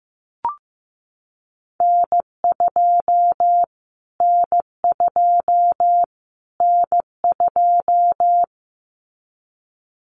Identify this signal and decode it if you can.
Morse code, heard as a keyed tone, about 15 wpm, 700 Hz. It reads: N2 N2 N2
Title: Callsign Prefixes for Top 25 DXCC - Repeated 3x - 15wpm